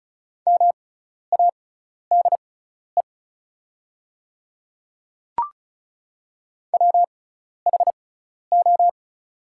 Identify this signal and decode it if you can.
Morse code, heard as a keyed tone, about 36 wpm, spaced as for 12 wpm, 700 Hz. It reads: MADE WHO
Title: 6x Character Spacing - Top 200 Words - 35wpm